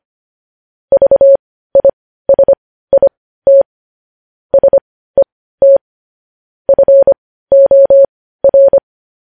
Morse code, heard as a keyed tone, about 25 wpm, spaced as for 15 wpm, 550 Hz. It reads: VISIT SET FOR
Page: News Headlines in Morse Code at 15 WPM